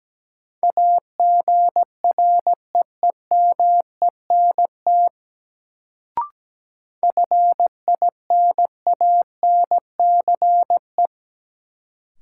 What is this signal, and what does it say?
Morse code, heard as a keyed tone, about 17 wpm, 700 Hz. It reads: AGREEMENT FINANCE
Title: Sentences from Top 1500 Words - New Word Drill - 17wpm